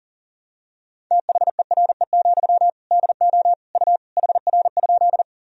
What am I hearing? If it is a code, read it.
Morse code, heard as a keyed tone, about 40 wpm, 700 Hz. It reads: THERE, DO U HR?
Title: Sentences from Top 300 Words and Abbreviations 40wpm